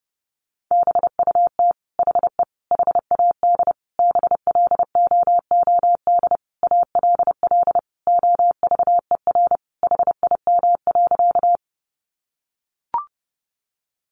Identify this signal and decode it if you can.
Morse code, heard as a keyed tone, about 30 wpm, 700 Hz. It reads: BUT HE HAD BLOOD ALL OVER HIM.
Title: Sentences from Top 1100 Words 30wpm